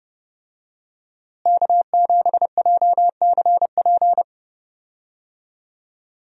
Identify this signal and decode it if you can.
Morse code, heard as a keyed tone, about 30 wpm, 700 Hz. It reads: K7JCP
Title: USA Call Signs - 5 Characters - Repeated 3x - 30wpm